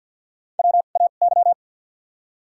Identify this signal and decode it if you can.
Morse code, heard as a keyed tone, about 54 wpm, 700 Hz. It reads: W A Y
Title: 2x Character Spacing - Top 500 Words - 50wpm